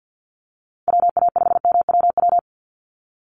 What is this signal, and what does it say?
Morse code, heard as a keyed tone, about 48 wpm, 700 Hz. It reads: WA5MWW
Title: USA Call Signs 50wpm